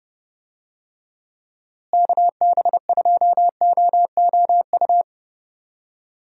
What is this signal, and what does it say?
Morse code, heard as a keyed tone, about 30 wpm, 700 Hz. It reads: KB2OOU
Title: USA Call Signs - 6 Characters - Repeated 2x - 30wpm